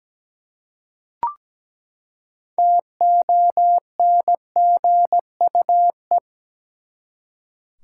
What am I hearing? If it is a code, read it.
Morse code, heard as a keyed tone, about 17 wpm, 700 Hz. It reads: TONGUE
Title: Six Letter Words - 17wpm